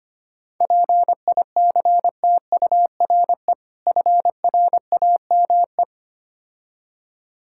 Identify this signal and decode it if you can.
Morse code, heard as a keyed tone, about 25 wpm, 690 Hz. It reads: PICTURE FRAME
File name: Easy Words - Speed-Racing with Hints - 25wpm